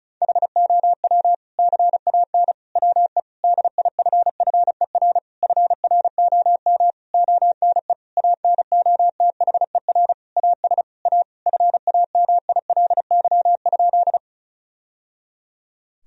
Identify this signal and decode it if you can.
Morse code, heard as a keyed tone, about 34 wpm, 700 Hz. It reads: HOW CAN WE DIFFER FROM ONE ANOTHER AS A FAMILY?
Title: Sentences from Top 1000 Words - Review Part 1 - 35wpm